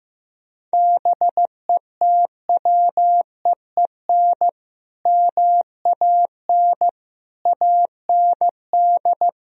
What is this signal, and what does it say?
Morse code, heard as a keyed tone, about 15 wpm, 700 Hz. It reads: BETWEEN MAN AND
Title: Sentences from Top 1000 Words - Review Part 1 - 15wpm